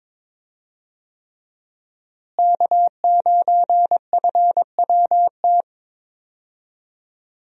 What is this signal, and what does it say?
Morse code, heard as a keyed tone, about 22 wpm, 700 Hz. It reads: K9FWT
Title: USA Call Signs - 5 Characters - Repeated 3x - 22wpm